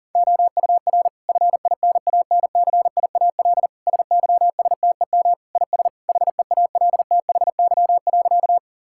Morse code, heard as a keyed tone, about 40 wpm, 700 Hz. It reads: OUR FINANCIAL SYSTEM IS HEALTHY.
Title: Sentences from Top 1000 Words - Review Part 5 - 40wpm